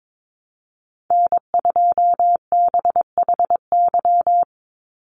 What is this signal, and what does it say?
Morse code, heard as a keyed tone, about 22 wpm, 700 Hz. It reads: N2BHY
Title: USA Call Signs 22wpm